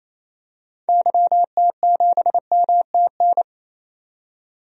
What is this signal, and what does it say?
Morse code, heard as a keyed tone, about 28 wpm, 700 Hz. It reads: YT7MTN
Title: DXCC-100 Call Signs - 6 Characters - 28wpm